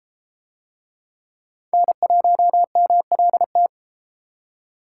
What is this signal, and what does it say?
Morse code, heard as a keyed tone, about 31 wpm, 700 Hz. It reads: N1MLT